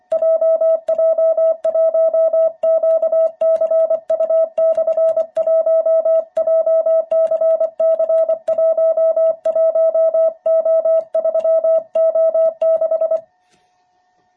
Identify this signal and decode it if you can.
Morse code, heard as a keyed tone, about 25 wpm, 660 Hz. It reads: JJ1QCU/1 JCC11O3O6
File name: jh8cbh / 2021cw